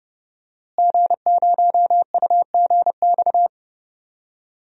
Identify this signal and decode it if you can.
Morse code, heard as a keyed tone, about 30 wpm, 700 Hz. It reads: G0UGX
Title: DXCC-100 Call Signs - 5 Characters - Repeated 2x - 30wpm